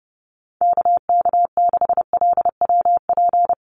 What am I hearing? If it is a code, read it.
Morse code, heard as a keyed tone, about 30 wpm, 700 Hz. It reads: KK6LWP
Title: USA Call Signs Repeated 2x 30wpm